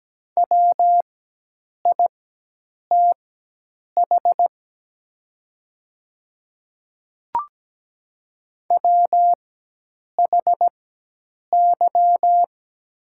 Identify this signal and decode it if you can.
Morse code, heard as a keyed tone, about 17 wpm, spaced as for 8 wpm, 700 Hz. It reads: WITH WHY
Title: 4x Character Spacing - Top 200 Words - 17wpm